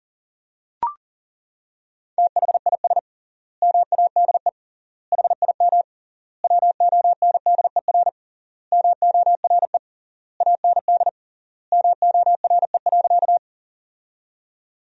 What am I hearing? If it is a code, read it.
Morse code, heard as a keyed tone, about 37 wpm, 700 Hz. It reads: THIS MADE HIM WONDER MORE AND MORE.
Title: 3x Word-Spacing - Sentences from Top 500 Words - 40wpm